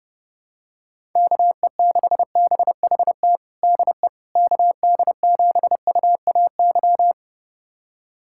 Morse code, heard as a keyed tone, about 30 wpm, 700 Hz. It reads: KE6BHT DE KD7UAY
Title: USA Callsign DE Callsign 30wpm